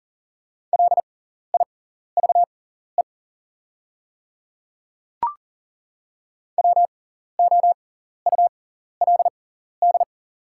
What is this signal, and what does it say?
Morse code, heard as a keyed tone, about 39 wpm, spaced as for 14 wpm, 690 Hz. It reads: LIVE WOULD
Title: 6x Character Spacing - Top 200 Words - 40wpm